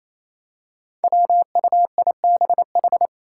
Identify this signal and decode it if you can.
Morse code, heard as a keyed tone, about 28 wpm, 700 Hz. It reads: WUIBH
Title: Five Challenging Letter-Number Combinations 28wpm